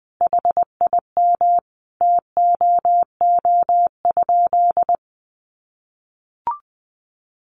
Morse code, heard as a keyed tone, about 20 wpm, 700 Hz. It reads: HIM TOO?
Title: Sentences from Top 500 Words 20wpm